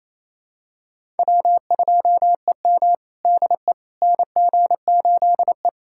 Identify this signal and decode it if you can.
Morse code, heard as a keyed tone, about 28 wpm, 700 Hz. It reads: W2EM DE NG8E
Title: USA Callsign DE Callsign - 4 Characters - 28wpm